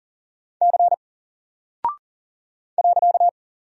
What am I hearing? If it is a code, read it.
Morse code, heard as a keyed tone, about 40 wpm, 700 Hz. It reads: C .
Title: Single Character (Letter:Number:Punctuation:Prosign) Alt 40wpm